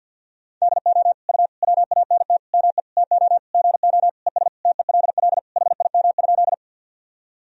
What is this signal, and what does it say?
Morse code, heard as a keyed tone, about 53 wpm, 700 Hz. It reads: DO U WANT ME TO GO ES TELL HIM?
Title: Sentences from Top 200 Words and Abbreviations 50wpm